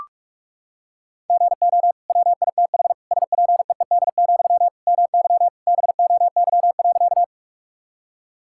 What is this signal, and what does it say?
Morse code, heard as a keyed tone, about 46 wpm, 700 Hz. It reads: GO WITH SPEED, MY BOY.
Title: Sentences from Top 1000 Words - Review Part 5 - 45wpm